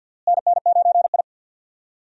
Morse code, heard as a keyed tone, about 46 wpm, 700 Hz. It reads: NN9I
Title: USA Call Signs - 4 Characters - Repeated 2x - 50wpm